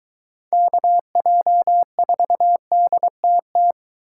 Morse code, heard as a keyed tone, about 22 wpm, 700 Hz. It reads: KJ4DTT